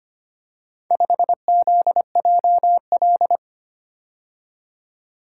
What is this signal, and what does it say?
Morse code, heard as a keyed tone, about 25 wpm, 700 Hz. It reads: 5ZJL
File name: practice / 25wpm / Four Letter-Number Combinations - 25wpm